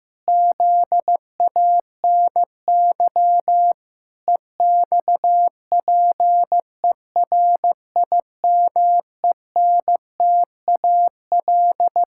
Morse code, heard as a keyed tone, about 14 wpm, 700 Hz. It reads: ZANY EXPERIMENTAL
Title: Pangram Challenge v2 - 15wpm